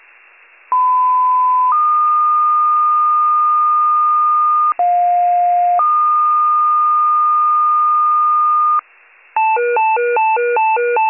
Two-tone page, heard as a one-second tone, then a three-second tone, about 1010 then 1220 Hz, about 710 then 1220 Hz. It is heard from 0.7 to 8.8 s.